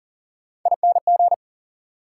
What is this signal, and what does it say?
Morse code, heard as a keyed tone, about 40 wpm, 700 Hz. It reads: ING
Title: Prefixes and Suffixes 40wpm